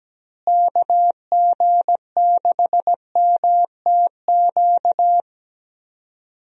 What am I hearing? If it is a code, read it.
Morse code, heard as a keyed tone, about 17 wpm, 700 Hz. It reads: KG6MTQ
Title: USA Call Signs - 6 Characters - Repeated 3x - 17wpm